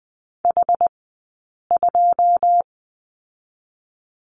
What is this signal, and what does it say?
Morse code, heard as a keyed tone, about 20 wpm, 700 Hz. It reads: H 2
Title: 120 Lesson 20d - Number 3 - Two Characters 20wpm